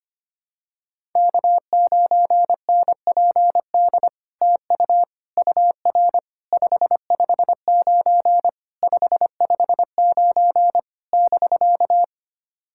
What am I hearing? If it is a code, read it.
Morse code, heard as a keyed tone, about 25 wpm, 700 Hz. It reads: K9NPD TU UR 559 559 <BK>
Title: SOTA QSO Practice 25wpm